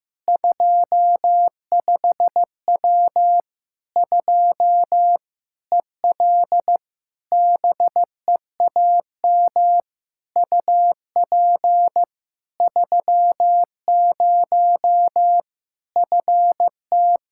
Morse code, heard as a keyed tone, about 15 wpm, 700 Hz. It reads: 25W 2 EL BEAM UP 30 FT
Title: Practice QSO - Exchange 2 - Info - 15wpm